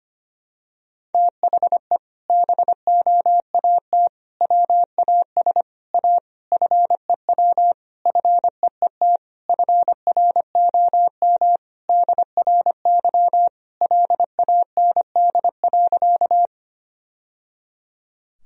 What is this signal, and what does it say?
Morse code, heard as a keyed tone, about 25 wpm, 700 Hz. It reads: THE BOAT WAS A FEW FEET FROM DRY LAND.
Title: Sentences from Top 1000 Words - Review Part 3 - 25wpm